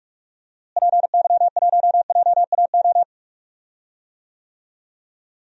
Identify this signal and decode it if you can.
Morse code, heard as a keyed tone, about 44 wpm, 700 Hz. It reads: PY1JAO
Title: DXCC-100 Call Signs - 6 Characters - Repeated 2x - 45wpm